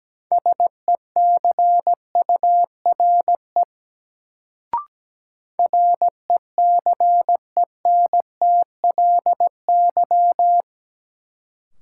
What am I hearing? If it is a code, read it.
Morse code, heard as a keyed tone, about 17 wpm, 700 Hz. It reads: SECURE RECENTLY
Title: Sentences from Top 1200 Words - New Word Drill - 17wpm